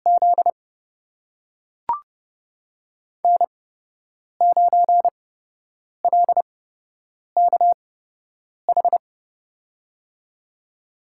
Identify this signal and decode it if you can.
Morse code, heard as a keyed tone, about 30 wpm, spaced as for 8 wpm, 700 Hz. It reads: Z N9LKH
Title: USA Call Signs - 8x Character Spacing - 30wpm